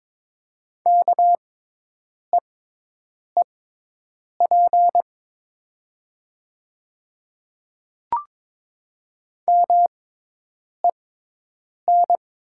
Morse code, heard as a keyed tone, about 22 wpm, spaced as for 8 wpm, 700 Hz. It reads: KEEP MEN
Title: 6x Character Spacing - Top 300 Words - 22wpm